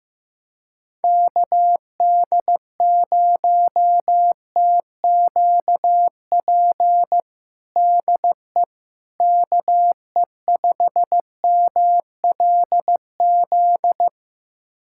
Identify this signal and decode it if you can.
Morse code, heard as a keyed tone, about 15 wpm, 700 Hz. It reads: KD0TQP DE KE5MLZ
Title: USA Callsign DE Callsign - 6 Characters - 15wpm